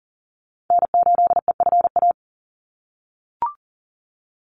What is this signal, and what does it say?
Morse code, heard as a keyed tone, about 40 wpm, 700 Hz. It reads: N8EFA
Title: USA Call Signs 40wpm